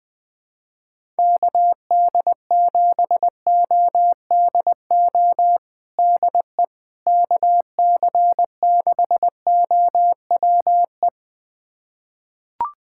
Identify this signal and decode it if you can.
Morse code, heard as a keyed tone, about 20 wpm, 700 Hz. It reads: KD7ODO DE KC6OWE
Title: USA Callsign DE Callsign - 6 Characters - 20wpm